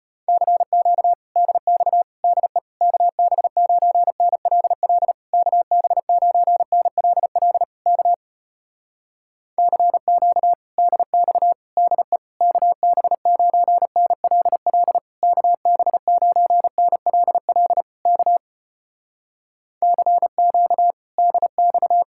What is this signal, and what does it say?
Morse code, heard as a keyed tone, about 35 wpm, 700 Hz. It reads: CQ DX DE KB9NLL KB9NLL K CQ DX DE KB9NLL KB9NLL K CQ DX